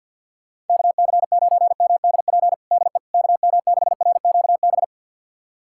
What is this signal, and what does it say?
Morse code, heard as a keyed tone, about 47 wpm, 690 Hz. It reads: KC9MDP DE KM6RQB
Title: USA Callsign DE Callsign - 6 Characters - 50wpm